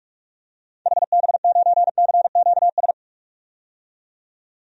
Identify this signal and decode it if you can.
Morse code, heard as a keyed tone, about 46 wpm, 700 Hz. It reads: HB9CQS
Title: DXCC-100 Call Signs - 6 Characters - 45wpm